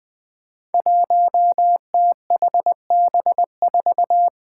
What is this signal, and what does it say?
Morse code, heard as a keyed tone, about 20 wpm, 700 Hz. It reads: 1THB4
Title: Five Challenging Letter-Number Combinations 20wpm